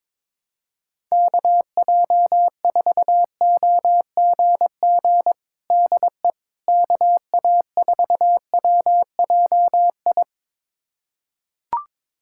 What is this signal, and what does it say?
Morse code, heard as a keyed tone, about 22 wpm, 700 Hz. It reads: KJ4OGG DE KA4WJI
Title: USA Callsign DE Callsign - 6 Characters - 22wpm